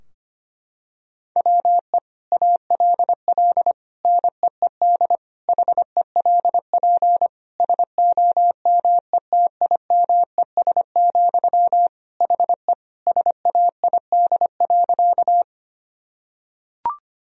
Morse code, heard as a keyed tone, about 25 wpm, 700 Hz. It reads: WE ALL NEED HELP SOMETIMES, HE SAID.
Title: Sentences from Top 1000 Words - Review Part 3 - 25wpm